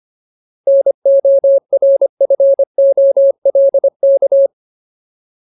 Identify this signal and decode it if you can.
Morse code, heard as a keyed tone, about 25 wpm, 550 Hz. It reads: NORFOLK
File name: Ultimate Word List / 25WPM Ultimate